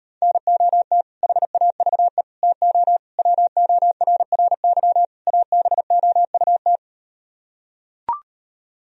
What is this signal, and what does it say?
Morse code, heard as a keyed tone, about 38 wpm, 700 Hz. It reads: NOT HAVE TO WORRY ABOUT